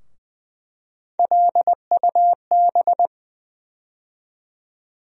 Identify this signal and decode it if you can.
Morse code, heard as a keyed tone, about 20 wpm, 700 Hz. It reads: LUB
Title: USA Call Sign Postfixes - 3 Letters 20wpm